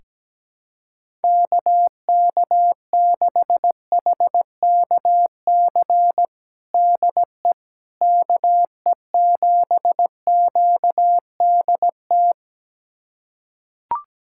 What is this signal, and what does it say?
Morse code, heard as a keyed tone, about 17 wpm, 700 Hz. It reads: KK6HKC DE KE7QDT
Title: USA Callsign DE Callsign - 6 Characters - 17wpm